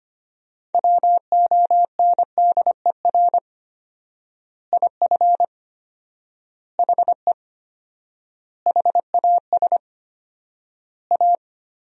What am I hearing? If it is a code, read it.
Morse code, heard as a keyed tone, about 25 wpm, 700 Hz. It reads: WONDER IF HE HAS A
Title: 4x Word-Spacing - Sentences from Top 500 Words - 25wpm